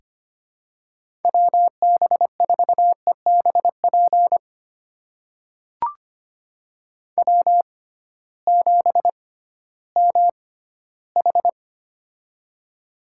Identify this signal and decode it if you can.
Morse code, heard as a keyed tone, about 25 wpm, spaced as for 14 wpm, 700 Hz. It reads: WB4EBP W7MH
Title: USA Call Signs - 6x Character Spacing - 25wpm